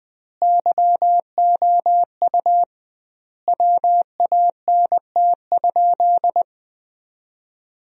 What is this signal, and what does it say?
Morse code, heard as a keyed tone, about 20 wpm, 700 Hz. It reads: YOU WANT?
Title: 2x Word-Spacing - Sentences from Top 400 Words - 20wpm